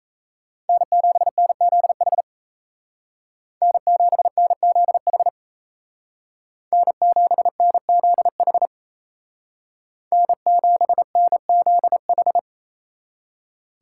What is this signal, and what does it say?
Morse code, heard as a keyed tone, about 37 wpm, 700 Hz. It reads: N7NZH N7NZH N7NZH N7NZH